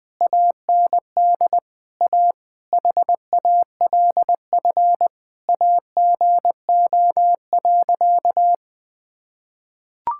Morse code, heard as a keyed tone, about 21 wpm, 700 Hz. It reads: AND A HALF AGO.
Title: Sentences with 7-Word Phrases - 20wpm